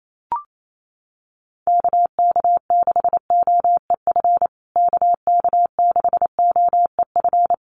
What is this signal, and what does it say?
Morse code, heard as a keyed tone, about 28 wpm, 700 Hz. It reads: KK6OEF KK6OEF
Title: USA Call Signs Repeated 3x 28wpm